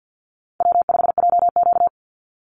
Morse code, heard as a keyed tone, about 50 wpm, 700 Hz. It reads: W5JQ
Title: Call Signs - 50wpm